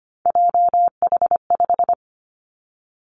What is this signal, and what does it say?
Morse code, heard as a keyed tone, about 25 wpm, 700 Hz. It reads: JH5
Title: Three Challenging Letter-Number Combinations 25wpm